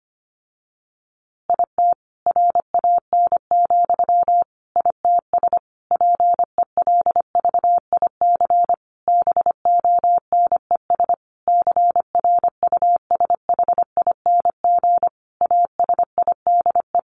Morse code, heard as a keyed tone, about 25 wpm, 700 Hz. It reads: IT RAN, ITS PELVIC BONES CRUSHING ASIDE
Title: A Sound Of Thunder 25wpm